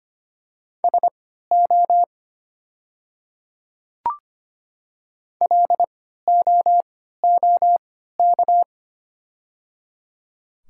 Morse code, heard as a keyed tone, about 25 wpm, spaced as for 14 wpm, 700 Hz. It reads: SO LOOK
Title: 3x Character Spacing - Top 100 Words - 25wpm